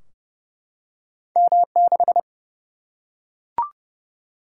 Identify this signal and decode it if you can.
Morse code, heard as a keyed tone, about 30 wpm, 700 Hz. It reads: M6
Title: Single-Letter Single-Number 30wpm